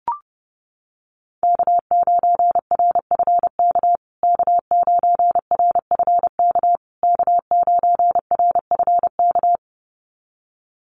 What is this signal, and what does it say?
Morse code, heard as a keyed tone, about 30 wpm, 700 Hz. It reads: K9RFK K9RFK K9RFK
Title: USA Call Signs Repeated 3x 30wpm